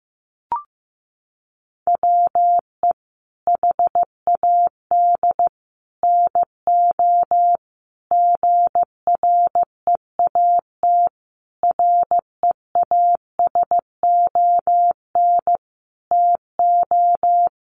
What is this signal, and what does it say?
Morse code, heard as a keyed tone, about 15 wpm, 700 Hz. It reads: WE HAD NO GREAT REASON TO
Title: Sentences from Top 1200 Words 15wpm